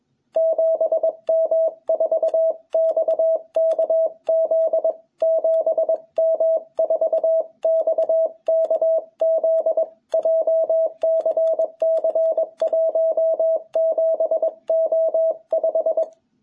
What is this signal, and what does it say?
Morse code, heard as a keyed tone, about 21 wpm, 660 Hz. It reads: 7M4XKZ 7M4XKZ JCC 17O5
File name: jh8cbh / 2021cw